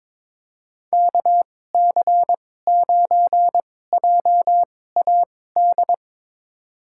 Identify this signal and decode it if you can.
Morse code, heard as a keyed tone, about 22 wpm, spaced as for 16 wpm, 700 Hz. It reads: KC9JAD
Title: USA Call Signs - 6 Characters - 2x Character Spacing - 22wpm